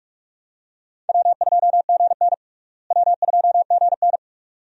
Morse code, heard as a keyed tone, about 46 wpm, 700 Hz. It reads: W2GN W2GN
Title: USA Call Signs - 4 Characters - Repeated 2x - 45wpm